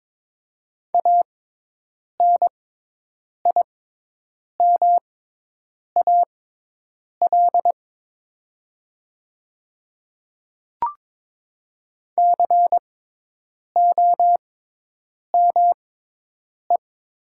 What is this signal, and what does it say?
Morse code, heard as a keyed tone, about 22 wpm, spaced as for 8 wpm, 700 Hz. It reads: ANIMAL COME
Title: 6x Character Spacing - Top 200 Words - 22wpm